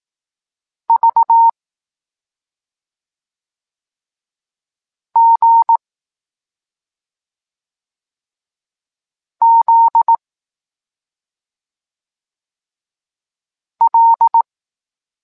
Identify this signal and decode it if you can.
Morse code, heard as a keyed tone, about 18 wpm, 920 Hz. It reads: V G Z L
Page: Tekst 23 – Doskonalenie odbioru poznanych liter